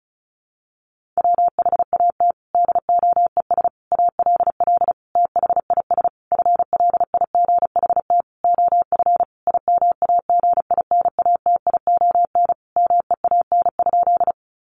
Morse code, heard as a keyed tone, about 34 wpm, 700 Hz. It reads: WHAT DOES ALL THIS FLIGHT OF IMAGINATION MEAN?
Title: Sentences from Top 1300 Words 35wpm